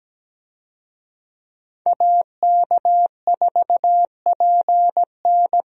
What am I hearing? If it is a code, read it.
Morse code, heard as a keyed tone, about 17 wpm, 700 Hz. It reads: AK4PN
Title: USA Call Signs - 5 Characters - Repeated 2x - 17wpm